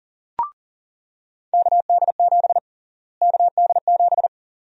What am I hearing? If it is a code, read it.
Morse code, heard as a keyed tone, about 37 wpm, 700 Hz. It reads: KD7 KD7
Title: USA Call Sign Prefixes - Repeated 2x - 40wpm